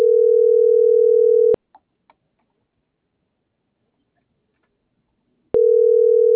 Ringback tone (North America), 440 + 480 Hz.